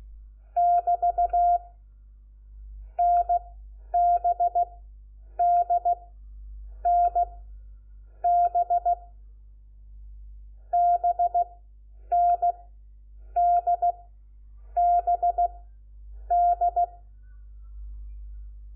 Morse code, heard as a keyed tone, about 16 wpm, 700 Hz. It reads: <BT> N B D N B B N D B D
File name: Lecciones / CLASE Nº4